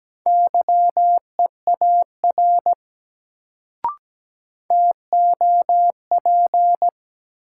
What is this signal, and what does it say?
Morse code, heard as a keyed tone, about 17 wpm, 700 Hz. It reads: YEAR TOP
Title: Sets of 2 Words - New Word Drill - 17wpm